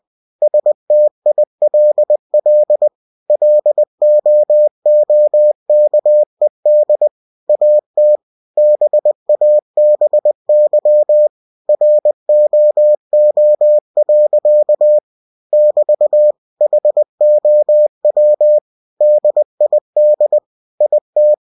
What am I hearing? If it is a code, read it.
Morse code, heard as a keyed tone, about 20 wpm, 600 Hz. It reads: STILL LOOKED AT BABY ROO. <BT> HOW DID IT